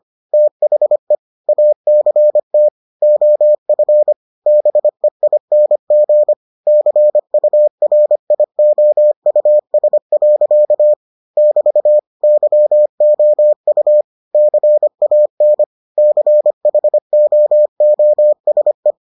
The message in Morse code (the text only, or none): THE ACT OF BEING CURIOUS. <BT> YOU CAN CHOOSE